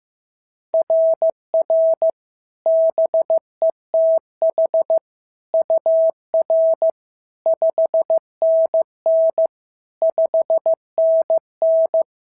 Morse code, heard as a keyed tone, about 14 wpm, 650 Hz. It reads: RR BETH UR 5NN 5NN
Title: SOTA QSO Practice 15wpm